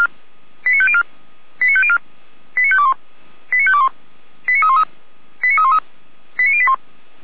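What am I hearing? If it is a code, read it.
Five-tone selective call (ZVEI), heard as quick runs of five tones, tones of about 70 ms. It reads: 89504 89504 89521 89521 89313 89313 89081
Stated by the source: Five-tone ZVEI selective call received on an SDR.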